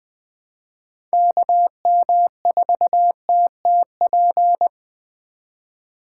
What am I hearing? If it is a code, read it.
Morse code, heard as a keyed tone, about 25 wpm, 700 Hz. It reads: KM4TTP
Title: Speed Racing - USA 2x3 Call Signs 20wpm